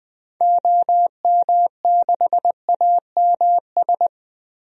Morse code, heard as a keyed tone, about 20 wpm, 700 Hz. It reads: OM6AMS
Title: DXCC-100 Call Signs - 6 Characters - Repeated 2x - 20wpm